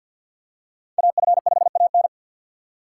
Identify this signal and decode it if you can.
Morse code, heard as a keyed tone, about 50 wpm, 700 Hz. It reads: AF5AN
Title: USA Call Signs - 4x Character Spacing - 50wpm